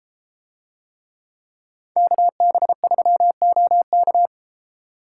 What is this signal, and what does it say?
Morse code, heard as a keyed tone, about 33 wpm, 700 Hz. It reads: KB3OK